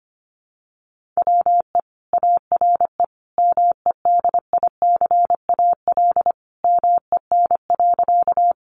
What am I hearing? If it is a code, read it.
Morse code, heard as a keyed tone, about 25 wpm, 700 Hz. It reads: WE ARE MEDICAL MEN.
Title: Sentences from Top 800 Words 25wpm